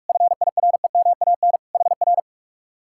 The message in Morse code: FIREMAN HR

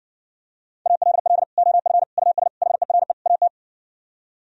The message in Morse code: ALL OF US HERE AT

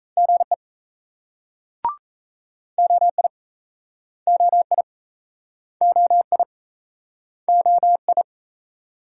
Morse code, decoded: GE OI OI OI OI

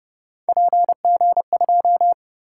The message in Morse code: PG2

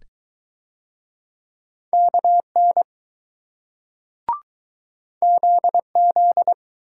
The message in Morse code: KN ZZ

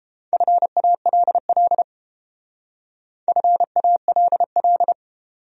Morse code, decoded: FALL FALL

